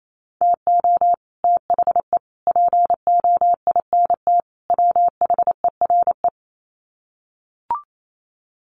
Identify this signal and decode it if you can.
Morse code, heard as a keyed tone, about 28 wpm, 700 Hz. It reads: TO THE POINT WHERE
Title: Sets of 4 Words 28wpm